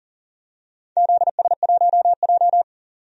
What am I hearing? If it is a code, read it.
Morse code, heard as a keyed tone, about 40 wpm, 700 Hz. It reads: ZS1J
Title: DXCC-100 Call Signs - 4 Characters - 40wpm